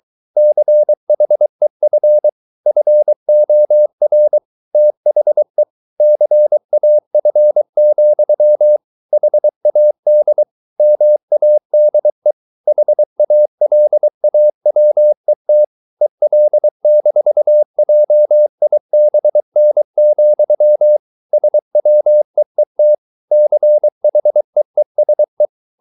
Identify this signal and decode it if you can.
Morse code, heard as a keyed tone, about 23 wpm, 600 Hz. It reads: CHEF FOR THE CAF, HAD MADE HALAWET EL-JIBN, SWEET CHEESE